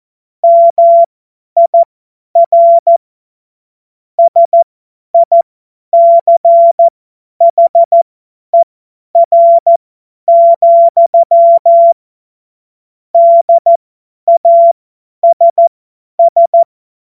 Morse code, heard as a keyed tone, about 14 wpm, spaced as for 10 wpm, 680 Hz. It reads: MIR SICHER, DASS
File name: Morsekurs / Witze / Witz 24. Januar 2023 50 F70